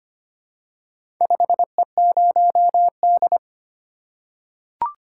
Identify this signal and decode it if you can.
Morse code, heard as a keyed tone, about 25 wpm, 700 Hz. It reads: 5E0D
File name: practice / 25wpm / Four Letter-Number Combinations - 25wpm